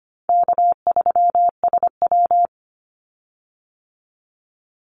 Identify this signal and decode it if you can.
Morse code, heard as a keyed tone, about 25 wpm, 700 Hz. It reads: K3SW
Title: Easy Call Signs Repeated 3x - 25wpm